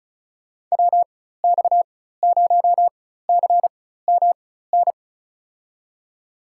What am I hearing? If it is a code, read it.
Morse code, heard as a keyed tone, about 36 wpm, spaced as for 16 wpm, 700 Hz. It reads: WX0CMN
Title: USA Call Signs - 6 Characters - 4x Character Spacing - 35wpm